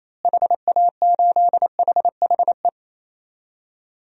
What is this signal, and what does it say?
Morse code, heard as a keyed tone, about 27 wpm, 700 Hz. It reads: HA8HHE
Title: DXCC-100 Call Signs - 6 Characters - Repeated 2x - 28wpm